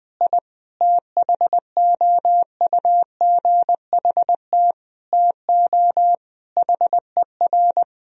Morse code, heard as a keyed tone, about 20 wpm, 700 Hz. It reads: I THOUGHT TO HER
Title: Sentences from Top 1000 Words - Review Part 2 - 20wpm